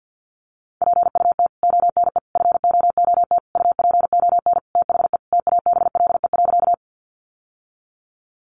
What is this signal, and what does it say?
Morse code, heard as a keyed tone, about 50 wpm, 700 Hz. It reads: PUT ONE FOOT UPON THE TABLE.
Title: Sentences from Top 1100 Words 50wpm